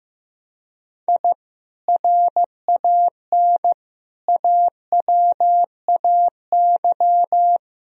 Morse code, heard as a keyed tone, about 16 wpm, 700 Hz. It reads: I RAN AWAY
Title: Sentences from Top 1000 Words - Review Part 3 - 15wpm